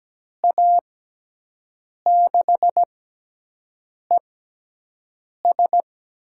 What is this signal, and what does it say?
Morse code, heard as a keyed tone, about 17 wpm, spaced as for 6 wpm, 700 Hz. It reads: A6ES
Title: USA Call Signs - 6x Character Spacing - 17wpm